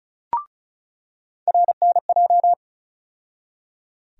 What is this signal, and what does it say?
Morse code, heard as a keyed tone, about 34 wpm, 700 Hz. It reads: RNJ
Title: USA Call Sign Postfixes - 3 Letters 35wpm